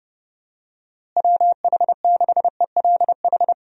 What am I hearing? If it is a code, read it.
Morse code, heard as a keyed tone, about 30 wpm, 700 Hz. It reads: WH6ELH
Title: USA Call Signs - 6 Characters - Repeated 2x - 30wpm